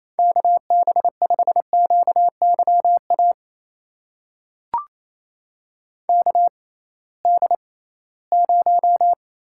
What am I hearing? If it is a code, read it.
Morse code, heard as a keyed tone, about 27 wpm, spaced as for 18 wpm, 700 Hz. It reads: KB5QYA KD0
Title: USA Call Signs - 6 Characters - 6x Character Spacing - 28wpm